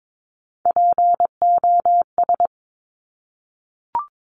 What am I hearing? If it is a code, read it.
Morse code, heard as a keyed tone, about 22 wpm, 700 Hz. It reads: POS
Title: Top Three Letter Combinations in Words 22wpm